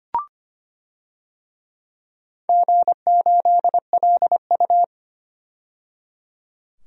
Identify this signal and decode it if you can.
Morse code, heard as a keyed tone, about 25 wpm, 700 Hz. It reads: G8LU
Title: DXCC-100 Call Signs - 4 Characters - 25wpm